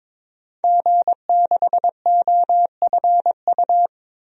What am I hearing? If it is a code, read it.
Morse code, heard as a keyed tone, about 22 wpm, 700 Hz. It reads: G6OFU
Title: DXCC-100 Call Signs - 5 Characters - Repeated 2x - 22wpm